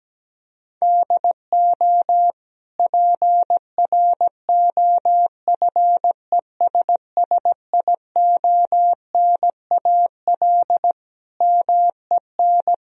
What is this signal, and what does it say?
Morse code, heard as a keyed tone, about 17 wpm, 700 Hz. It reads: DO PROFESSIONAL MEN
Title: Sentences from Top 1000 Words - Review Part 4 - 17wpm